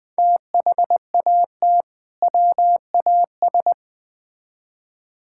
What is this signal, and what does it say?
Morse code, heard as a keyed tone, about 21 wpm, 700 Hz. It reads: THAT WAS
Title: Build A Phrase - 2-Word Phrases - 20wpm